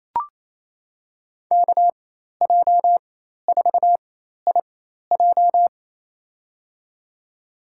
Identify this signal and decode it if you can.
Morse code, heard as a keyed tone, about 28 wpm, spaced as for 13 wpm, 700 Hz. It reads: KJ4IJ
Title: USA Call Signs - 5 Characters - 4x Character Spacing - 28wpm